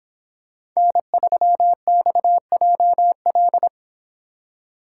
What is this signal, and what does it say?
Morse code, heard as a keyed tone, about 26 wpm, 700 Hz. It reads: N3XJL
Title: Speed Racing - USA 1x3 Call Signs 22wpm